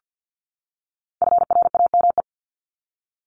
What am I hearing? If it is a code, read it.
Morse code, heard as a keyed tone, about 50 wpm, 700 Hz. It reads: FRAME
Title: Top 751-1000 Words 50wpm